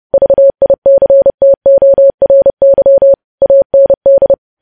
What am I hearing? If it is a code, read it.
Morse code, heard as a keyed tone, about 30 wpm, 550 Hz. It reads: VICTORY AND
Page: news headlines in morse code at 30 wpm